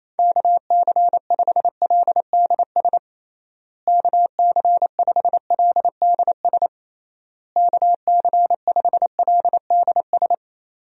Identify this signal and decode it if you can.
Morse code, heard as a keyed tone, about 27 wpm, 700 Hz. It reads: KC5LDS KC5LDS KC5LDS
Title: USA Call Signs - 6 Characters - Repeated 3x - 28wpm